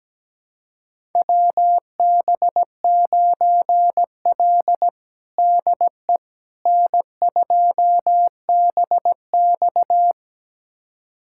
Morse code, heard as a keyed tone, about 17 wpm, 700 Hz. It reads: WB9L DE N2BX
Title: USA Callsign DE Callsign - 4 Characters - 17wpm